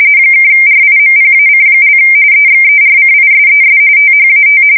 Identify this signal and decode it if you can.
RTTY, about 45.45 baud. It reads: (RIMPAC) EXERCISES THAT
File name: fsk170 / USS-Missouri